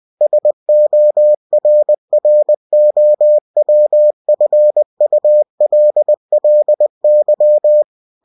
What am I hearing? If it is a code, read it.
Morse code, heard as a keyed tone, about 20 wpm, 600 Hz. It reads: SORROWFULLY
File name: cw-books / winnie-the-pooh / winnie-20wpm-page-0058